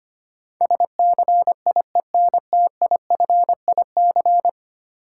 Morse code, heard as a keyed tone, about 25 wpm, 700 Hz. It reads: SCIENTIFIC
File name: Ten Letter Words - 25wpm